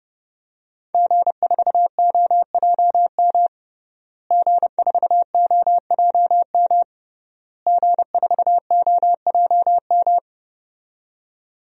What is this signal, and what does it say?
Morse code, heard as a keyed tone, about 32 wpm, 700 Hz. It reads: G4OJM G4OJM G4OJM
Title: DXCC-100 Call Signs - 5 Characters - Repeated 3x - 30wpm